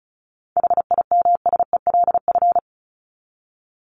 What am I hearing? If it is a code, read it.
Morse code, heard as a keyed tone, about 34 wpm, 700 Hz. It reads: HIMSELF
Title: Top 1501-1750 Words 35wpm